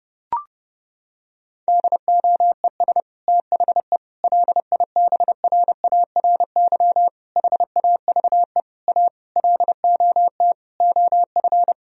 Morse code, heard as a keyed tone, about 32 wpm, 700 Hz. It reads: DOES THE LIBRARY HAVE A LOT OF
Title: Sentences from Top 1000 Words - Review Part 4 - 30wpm